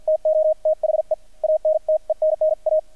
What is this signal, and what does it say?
Morse code, heard as a keyed tone, about 50 wpm, spaced as for 43 wpm, 620 Hz, with radio noise. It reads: TO THE ANTENNA
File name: archived / CRSnet